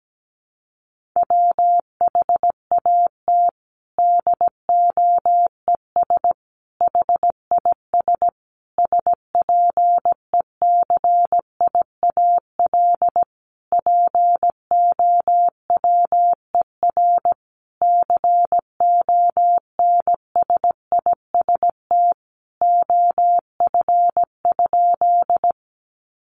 Morse code, heard as a keyed tone, about 17 wpm, 700 Hz. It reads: WHAT DOES HIS SPECIAL POWER CONSIST OF?
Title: Sentences from Top 1400 Words 17wpm